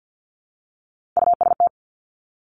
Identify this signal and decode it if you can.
Morse code, heard as a keyed tone, about 50 wpm, 700 Hz. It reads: UST
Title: Top Three Letter Combinations in Words 50wpm